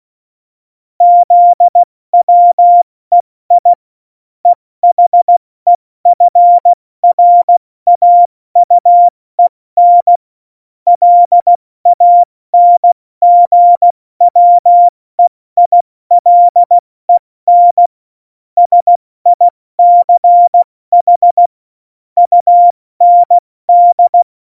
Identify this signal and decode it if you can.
Morse code, heard as a keyed tone, about 16 wpm, spaced as for 14 wpm, 700 Hz. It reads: ZWEI EHEFRAUEN LANGWEILEN SICH UND
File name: Morsekurs / Witze / Witz 18 August 2020, 70 F80